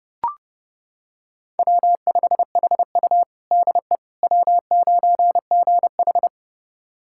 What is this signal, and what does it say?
Morse code, heard as a keyed tone, about 30 wpm, 700 Hz. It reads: W5HU DE W9GH
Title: USA Callsign DE Callsign - 4 Characters - 30wpm